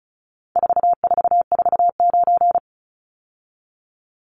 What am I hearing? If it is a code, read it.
Morse code, heard as a keyed tone, about 34 wpm, 700 Hz. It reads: 4449
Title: Numbers - 4-Digit 35wpm